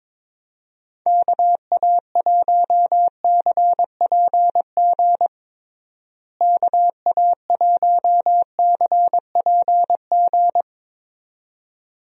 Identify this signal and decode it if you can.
Morse code, heard as a keyed tone, about 22 wpm, 700 Hz. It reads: KA1CPG KA1CPG
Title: USA Call Signs - 6 Characters - Repeated 2x - 22wpm